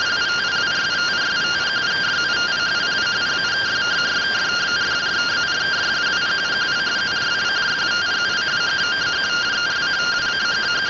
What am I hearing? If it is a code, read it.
Two-tone teleprinter signal, about 170 Hz shift.